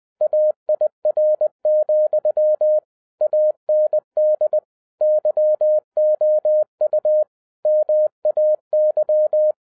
Morse code, heard as a keyed tone, about 20 wpm, 610 Hz. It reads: AIR, AND YOU MAY
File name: 2012-01-mp3 / n15-INVESTIGATION INTO THE POSSIBILITY OF WINNING PREFIX VERSION 3 AWARD-20wpm